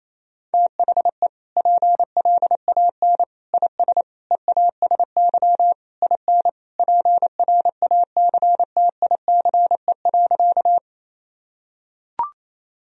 Morse code, heard as a keyed tone, about 28 wpm, 700 Hz. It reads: THE PLAN IS EASY IN PRACTICE.
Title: Sentences from Top 1000 Words - Review Part 4 - 28wpm